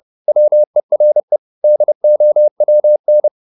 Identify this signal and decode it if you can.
Morse code, heard as a keyed tone, about 30 wpm, 600 Hz. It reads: WERE DOWN
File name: the old man and the sea / the old man and the sea 1 30wpm